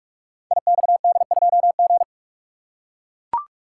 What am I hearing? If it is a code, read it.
Morse code, heard as a keyed tone, about 44 wpm, 700 Hz. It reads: IXD2G
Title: Code Groups - 5 Letter:Numbers - 45wpm